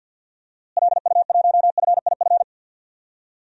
Morse code, heard as a keyed tone, about 46 wpm, 700 Hz. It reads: LU1FIF
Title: DXCC-100 Call Signs - 6 Characters - Repeated 2x - 50wpm